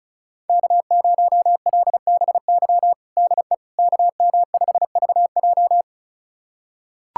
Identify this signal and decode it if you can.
Morse code, heard as a keyed tone, about 36 wpm, 700 Hz. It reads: K0LBY DE KM5VJ
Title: USA Callsign DE Callsign - 5 Characters - 35wpm